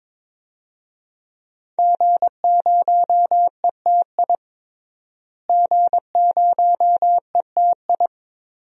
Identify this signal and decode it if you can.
Morse code, heard as a keyed tone, about 22 wpm, 700 Hz. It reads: G0ETI G0ETI
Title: DXCC-100 Call Signs - 5 Characters - Repeated 3x - 22wpm